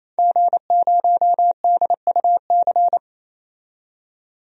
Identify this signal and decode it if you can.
Morse code, heard as a keyed tone, about 28 wpm, 700 Hz. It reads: G0DUC
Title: DXCC-100 Call Signs - 5 Characters - 28wpm